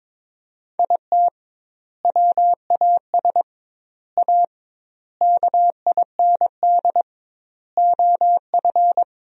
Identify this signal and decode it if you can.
Morse code, heard as a keyed tone, about 22 wpm, 700 Hz. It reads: IT WAS A KIND OF
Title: 2x Word-Spacing - Sentences from Top 500 Words - 22wpm